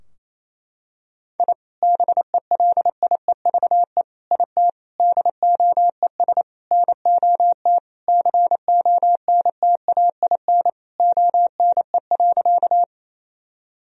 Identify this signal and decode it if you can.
Morse code, heard as a keyed tone, about 27 wpm, 700 Hz. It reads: I BELIEVE IT DOES NOT CONTAIN ONE.
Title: Sentences from Top 1000 Words - Review Part 2 - 28wpm